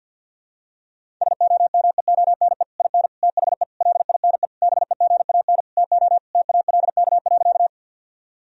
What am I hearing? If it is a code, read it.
Morse code, heard as a keyed tone, about 52 wpm, 700 Hz. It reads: SOMEONE IN THE LINE BEGAN TO TALK.
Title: Sentences from Top 1000 Words - Review Part 1 - 50wpm